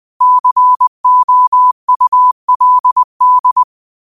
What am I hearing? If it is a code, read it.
Morse code, heard as a keyed tone, about 20 wpm, 1000 Hz. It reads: COULD
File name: qotdfiles / 20wpm